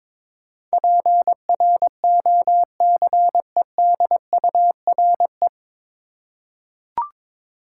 Morse code, heard as a keyed tone, about 22 wpm, 700 Hz. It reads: PROCEDURE
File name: 9 Letter Words - First 3 Letters Given - 22wpm